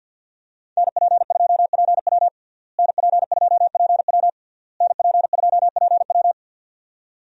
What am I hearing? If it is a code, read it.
Morse code, heard as a keyed tone, about 48 wpm, 700 Hz. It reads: NP2PW NP2PW NP2PW
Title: USA Call Signs - 5 Characters - Repeated 3x - 50wpm